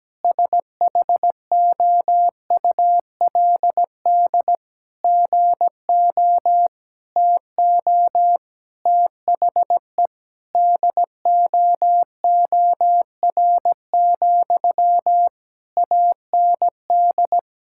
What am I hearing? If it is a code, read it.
Morse code, heard as a keyed tone, about 17 wpm, 700 Hz. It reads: SHOULD GO TO THE DOOR, AND
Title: Sentences from Top 1600 Words 17wpm